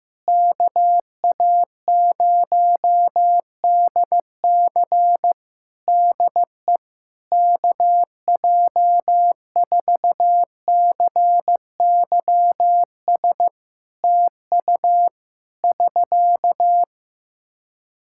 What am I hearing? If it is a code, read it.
Morse code, heard as a keyed tone, about 16 wpm, 700 Hz. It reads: KA0DC DE KJ4CYS TU <SK>
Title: K1USN SST QSO Practice 15wpm